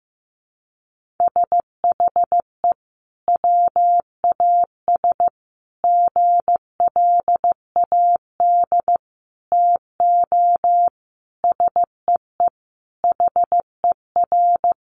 Morse code, heard as a keyed tone, about 15 wpm, 700 Hz. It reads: SHE WAS GLAD TO SEE HER
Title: Sentences from Top 800 Words 15wpm